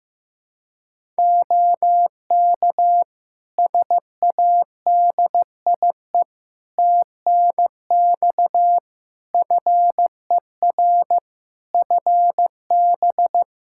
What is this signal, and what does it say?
Morse code, heard as a keyed tone, about 15 wpm, 700 Hz. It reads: OK SADIE TNX FER FB
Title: Practice QSO - Exchange 4 - Ending - 15wpm